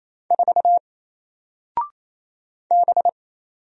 Morse code, heard as a keyed tone, about 28 wpm, 700 Hz. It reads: 4 B
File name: Single-Letter-Number-28wpm